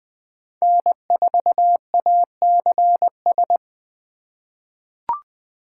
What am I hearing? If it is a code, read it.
Morse code, heard as a keyed tone, about 20 wpm, 700 Hz. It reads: N4ACS
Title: USA Call Signs - 5 Characters - 2x Character Spacing - 20wpm